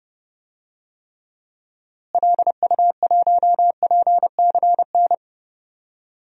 Morse code, heard as a keyed tone, about 29 wpm, 700 Hz. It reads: LU1PCN
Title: DXCC-100 Call Signs - 6 Characters - Repeated 3x - 30wpm